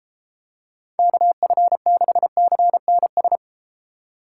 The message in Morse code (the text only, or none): KF6CNS